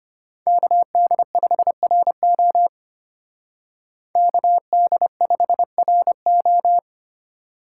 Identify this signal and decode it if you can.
Morse code, heard as a keyed tone, about 31 wpm, 700 Hz. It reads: KD5RO KD5RO
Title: Speed Racing - USA 2x2 Call Signs 25wpm